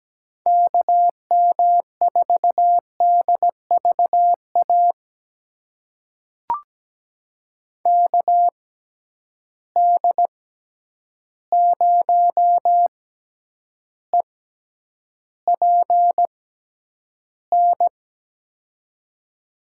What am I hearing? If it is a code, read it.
Morse code, heard as a keyed tone, about 17 wpm, spaced as for 9 wpm, 700 Hz. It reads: KM4DVA KD0EPN
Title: USA Call Signs - 6 Characters - 6x Character Spacing - 17wpm